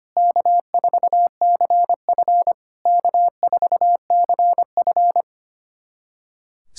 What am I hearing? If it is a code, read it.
Morse code, heard as a keyed tone, about 25 wpm, 690 Hz. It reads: K4CF K4CF
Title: SOTA QSO Practice 25wpm